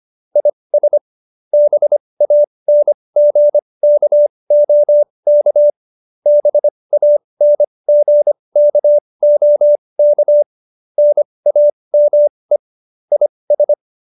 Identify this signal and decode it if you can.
Morse code, heard as a keyed tone, about 25 wpm, spaced as for 20 wpm, 590 Hz. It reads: IS BANGKOK BANGKOK NAME IS
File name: qso216b 20